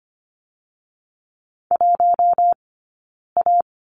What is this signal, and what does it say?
Morse code, heard as a keyed tone, about 25 wpm, spaced as for 9 wpm, 700 Hz. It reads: 1A
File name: Lesson 17d - Letter u - Two Characters 25wpm